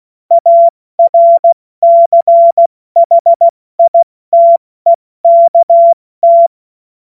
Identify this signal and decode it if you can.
Morse code, heard as a keyed tone, about 16 wpm, spaced as for 14 wpm, 680 Hz. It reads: ARCHITEKT
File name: Morsekurs / Witze / Witz 15. August 2023 70 F80